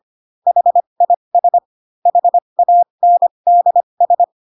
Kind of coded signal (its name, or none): Morse code